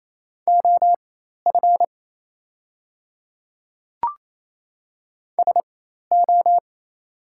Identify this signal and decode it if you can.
Morse code, heard as a keyed tone, about 28 wpm, spaced as for 13 wpm, 700 Hz. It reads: OF SO
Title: 4x Character Spacing - Top 300 Words - 28wpm